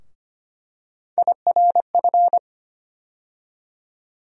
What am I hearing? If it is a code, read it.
Morse code, heard as a keyed tone, about 25 wpm, 700 Hz. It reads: IRF
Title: USA Call Sign Postfixes - 3 Letters 25wpm